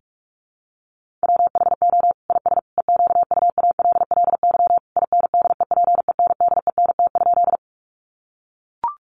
Morse code, heard as a keyed tone, about 44 wpm, 700 Hz. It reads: WHO IS EQUALLY INDEPENDENT?
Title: Sentences from Top 1400 Words 45wpm